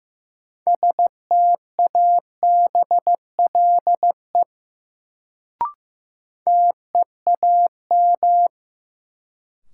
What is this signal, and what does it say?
Morse code, heard as a keyed tone, about 15 wpm, 700 Hz. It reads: STABLE TEAM
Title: Sentences from Top 1500 Words - New Word Drill - 15wpm